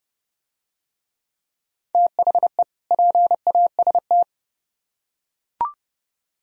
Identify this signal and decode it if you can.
Morse code, heard as a keyed tone, about 29 wpm, 700 Hz. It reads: THE PAST